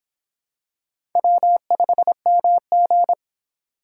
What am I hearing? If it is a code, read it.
Morse code, heard as a keyed tone, about 25 wpm, 700 Hz. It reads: W5MG